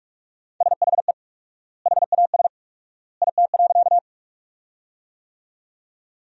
Morse code, heard as a keyed tone, about 46 wpm, 700 Hz. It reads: SHE HAS IT.
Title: 4x Word-Spacing - Sentences from Top 100 Words - 45wpm